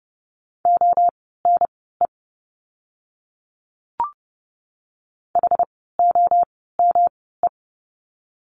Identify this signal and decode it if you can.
Morse code, heard as a keyed tone, about 30 wpm, spaced as for 17 wpm, 700 Hz. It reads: ONE HOME
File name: course / 30:10wpm / 115 Lesson 19f - Letter w - Words 30wpm